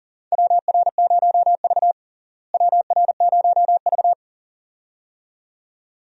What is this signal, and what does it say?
Morse code, heard as a keyed tone, about 40 wpm, 700 Hz. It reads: WR0V WR0V
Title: USA Call Signs - 4 Characters - Repeated 2x - 40wpm